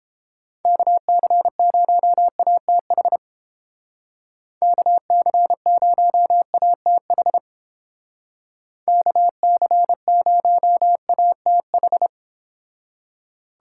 Morse code, heard as a keyed tone, about 30 wpm, 700 Hz. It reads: KC0ATH KC0ATH KC0ATH